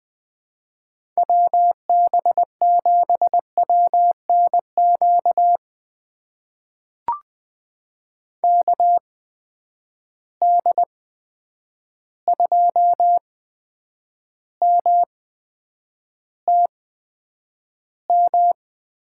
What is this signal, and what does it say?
Morse code, heard as a keyed tone, about 20 wpm, spaced as for 9 wpm, 700 Hz. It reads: WB7WNQ KD2MTM
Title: USA Call Signs - 6 Characters - 8x Character Spacing - 20wpm